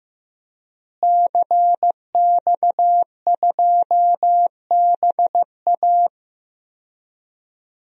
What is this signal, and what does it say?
Morse code, heard as a keyed tone, about 15 wpm, 700 Hz. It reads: CX2BA
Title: DXCC-100 Call Signs - 5 Characters - Repeated 2x - 15wpm